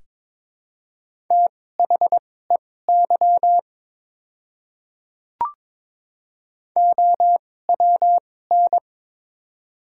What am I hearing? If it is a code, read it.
Morse code, heard as a keyed tone, about 22 wpm, spaced as for 16 wpm, 700 Hz. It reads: THEY OWN
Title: 2x Character Spacing - Top 300 Words - 22wpm